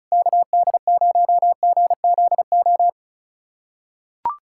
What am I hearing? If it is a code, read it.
Morse code, heard as a keyed tone, about 36 wpm, 700 Hz. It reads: KD0GZO
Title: USA Callsign DE Callsign 35wpm